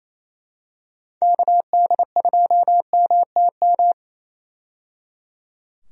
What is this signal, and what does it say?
Morse code, heard as a keyed tone, about 28 wpm, 700 Hz. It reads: KD2MTM
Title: USA Call Signs - 6 Characters - 28wpm